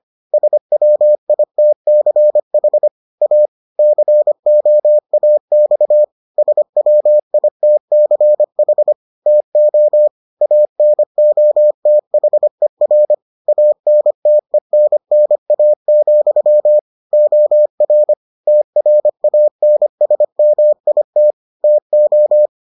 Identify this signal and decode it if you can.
Morse code, heard as a keyed tone, about 25 wpm, 600 Hz. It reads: SWITCH A COAX SWITCH TO ANOTHER ANTENNA, OR TRANSMIT TO